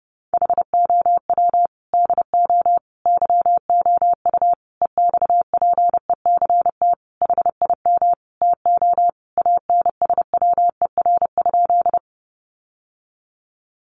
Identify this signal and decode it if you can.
Morse code, heard as a keyed tone, about 30 wpm, 700 Hz. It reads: HOW DO YOU EXPECT HIM TO ANSWER?
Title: Sentences from Top 1100 Words 30wpm